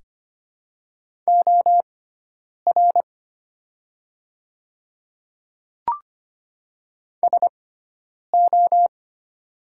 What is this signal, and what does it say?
Morse code, heard as a keyed tone, about 25 wpm, spaced as for 9 wpm, 700 Hz. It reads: OR SO
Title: 6x Character Spacing - Top 100 Words - 25wpm